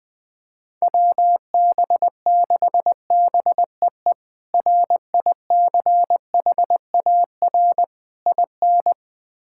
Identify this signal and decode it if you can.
Morse code, heard as a keyed tone, about 20 wpm, 700 Hz. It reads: WB6BEE RICHAR IN
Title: K1USN SST QSO Practice - 20wpm